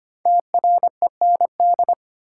Morse code, heard as a keyed tone, about 25 wpm, 700 Hz. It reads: TREND
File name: 5 Letter Words - First 3 Letters Given - 25wpm